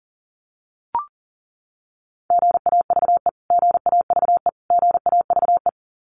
Morse code, heard as a keyed tone, about 40 wpm, 700 Hz. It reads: GAVE GAVE GAVE
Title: Top 500 Words Repeated 40wpm